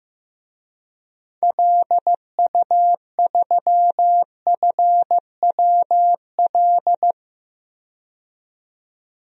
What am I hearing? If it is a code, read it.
Morse code, heard as a keyed tone, about 15 wpm, 700 Hz. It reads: LU3FWL
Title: DXCC-100 Call Signs - 6 Characters - 15wpm